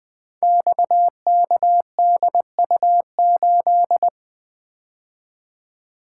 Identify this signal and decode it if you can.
Morse code, heard as a keyed tone, about 20 wpm, 700 Hz. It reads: XKDU8
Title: Code Groups - 5 Letter:Numbers - 20wpm